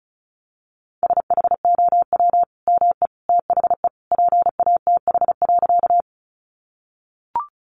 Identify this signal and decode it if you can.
Morse code, heard as a keyed tone, about 34 wpm, 700 Hz. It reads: SHOW ME THE PATH.
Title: Sentences from Top 1400 Words 35wpm